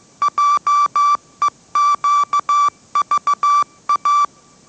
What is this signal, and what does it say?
Morse code, heard as a keyed tone, about 15 wpm, 1160 Hz. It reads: JEQVA
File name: mrx / mrx065 16wpm